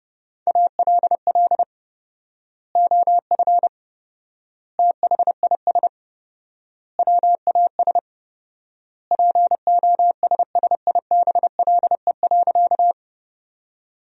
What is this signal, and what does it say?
Morse code, heard as a keyed tone, about 30 wpm, 700 Hz. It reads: ALL OF THIS WAS POSSIBLE.
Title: 4x Word-Spacing - Sentences from Top 500 Words - 30wpm